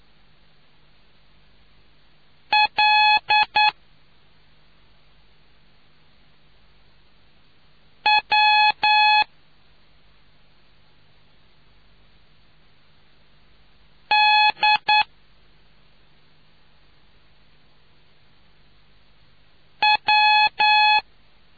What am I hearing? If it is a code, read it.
Morse code, heard as a keyed tone, about 9 wpm, 850 Hz. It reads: L W D W